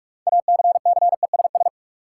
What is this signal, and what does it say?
Morse code, heard as a keyed tone, about 42 wpm, 700 Hz. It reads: ACCESS
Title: Six Letter Words - 45wpm